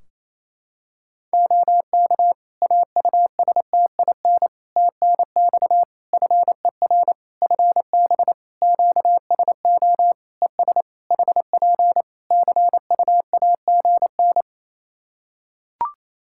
Morse code, heard as a keyed tone, about 28 wpm, 700 Hz. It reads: OK AUSTIN TNX FER FB QSO ES HP CUAGN
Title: Practice QSO - Exchange 4 - Ending - 28wpm